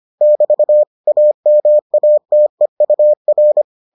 Morse code, heard as a keyed tone, about 25 wpm, 600 Hz. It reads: <BT> AMATEUR